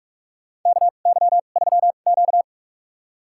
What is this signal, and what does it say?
Morse code, heard as a keyed tone, about 44 wpm, spaced as for 32 wpm, 700 Hz. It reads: KY3Q
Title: USA Call Signs - 4 Characters - 2x Character Spacing - 45wpm